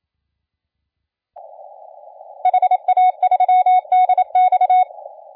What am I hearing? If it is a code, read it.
Morse code, heard as a keyed tone, about 27 wpm, 740 Hz. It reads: HA3DX
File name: la4c / sac-cw-2017 / CONTEST AUDIO 1100Z